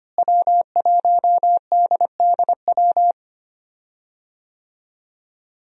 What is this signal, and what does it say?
Morse code, heard as a keyed tone, about 25 wpm, 700 Hz. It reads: W1DDW
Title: USA Call Signs - 5 Characters - Repeated 2x - 25wpm